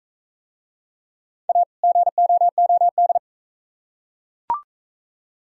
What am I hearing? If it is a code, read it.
Morse code, heard as a keyed tone, about 36 wpm, 700 Hz. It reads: A GOOD